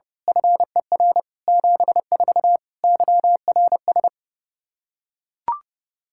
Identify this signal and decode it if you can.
Morse code, heard as a keyed tone, about 30 wpm, 700 Hz. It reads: FER 74 YRS
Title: K1USN SST QSO Practice 30wpm